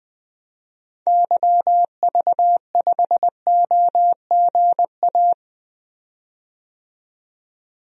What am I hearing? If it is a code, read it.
Morse code, heard as a keyed tone, about 20 wpm, 700 Hz. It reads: YV5OGA
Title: DXCC-100 Call Signs - 6 Characters - 20wpm